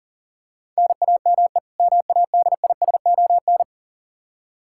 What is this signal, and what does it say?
Morse code, heard as a keyed tone, about 40 wpm, 700 Hz. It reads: NAME MADISON
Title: Practice QSO - Exchange 1 - Beginning - 40wpm